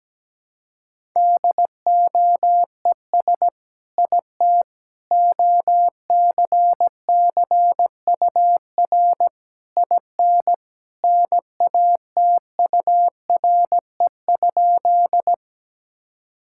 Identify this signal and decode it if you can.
Morse code, heard as a keyed tone, about 17 wpm, 700 Hz. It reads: DOES IT OCCUR IN NATURE?
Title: Sentences from Top 1000 Words - Review Part 6 - 17wpm